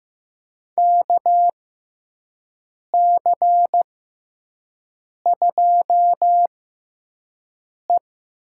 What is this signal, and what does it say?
Morse code, heard as a keyed tone, about 15 wpm, spaced as for 5 wpm, 700 Hz. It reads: KC2E
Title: USA Call Signs - 6 Characters - 6x Character Spacing - 15wpm